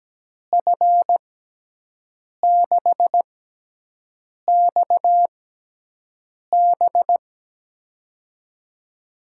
Morse code, heard as a keyed tone, about 17 wpm, spaced as for 6 wpm, 700 Hz. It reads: F6XB
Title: USA Call Signs - 5 Characters - 6x Character Spacing - 17wpm